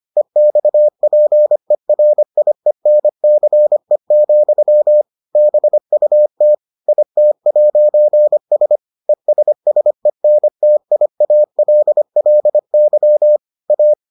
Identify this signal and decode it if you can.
Morse code, heard as a keyed tone, about 25 wpm, 600 Hz. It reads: EXPERIENCE, BUT IT'S ESSENTIALLY A